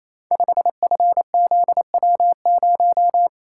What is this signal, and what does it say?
Morse code, heard as a keyed tone, about 29 wpm, 700 Hz. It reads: 5FZW0
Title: Code Groups - 5 Letter:Numbers - 28wpm